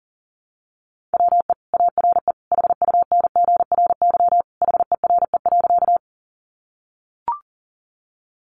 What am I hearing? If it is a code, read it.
Morse code, heard as a keyed tone, about 40 wpm, 700 Hz. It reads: WE ARE HUNGRY HERE.
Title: Sentences from Top 1400 Words 40wpm